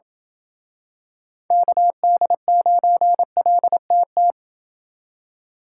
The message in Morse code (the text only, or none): KD9LTT